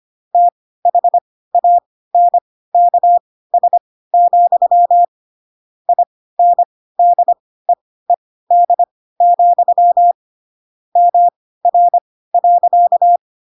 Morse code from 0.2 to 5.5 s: THANKS,